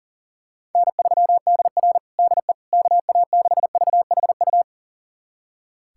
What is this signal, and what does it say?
Morse code, heard as a keyed tone, about 40 wpm, 700 Hz. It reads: N3DR DE KA6VHU
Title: USA Callsign DE Callsign 40wpm